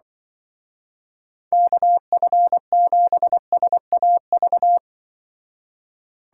Morse code, heard as a keyed tone, about 25 wpm, 700 Hz. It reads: KF7SAV
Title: Speed Racing - USA 2x3 Call Signs 20wpm